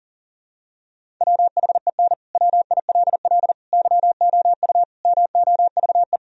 Morse code, heard as a keyed tone, about 40 wpm, 700 Hz. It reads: WHEN WILL YOU MOVE
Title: Sentences from Top 1000 Words - Review Part 1 - 40wpm